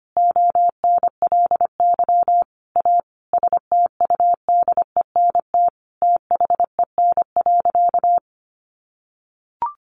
Morse code, heard as a keyed tone, about 25 wpm, 700 Hz. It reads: ONLY A STUDENT THEN.
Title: Sentences from Top 1000 Words 25wpm